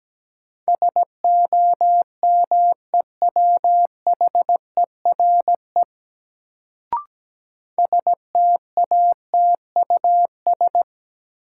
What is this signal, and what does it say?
Morse code, heard as a keyed tone, about 17 wpm, 700 Hz. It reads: SOMEWHERE STATUS
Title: Sentences from Top 1400 Words - New Word Drill Alt - 17wpm